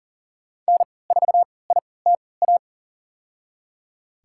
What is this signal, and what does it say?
Morse code, heard as a keyed tone, about 40 wpm, spaced as for 23 wpm, 700 Hz. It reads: N4ITA
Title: USA Call Signs - 5 Characters - 3x Character Spacing - 40wpm